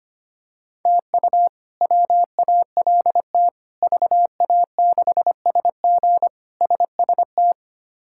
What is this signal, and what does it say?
Morse code, heard as a keyed tone, about 25 wpm, 700 Hz. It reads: TU WALT VA6SG SST
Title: K1USN SST QSO Practice - 25wpm